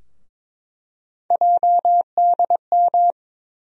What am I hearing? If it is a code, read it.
Morse code, heard as a keyed tone, about 22 wpm, 700 Hz. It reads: JDM
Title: USA Call Sign Postfixes - 3 Letters 22wpm